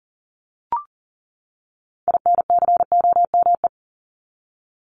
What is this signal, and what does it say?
Morse code, heard as a keyed tone, about 40 wpm, 700 Hz. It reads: INCOME
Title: Top 751-1000 Words 40wpm